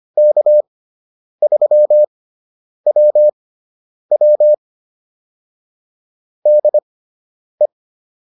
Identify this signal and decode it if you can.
Morse code, heard as a keyed tone, about 25 wpm, spaced as for 9 wpm, 600 Hz. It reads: K3WW DE